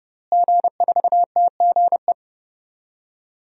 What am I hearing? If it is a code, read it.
Morse code, heard as a keyed tone, about 32 wpm, 700 Hz. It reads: G4TGE
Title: DXCC-100 Call Signs - 5 Characters - Repeated 3x - 30wpm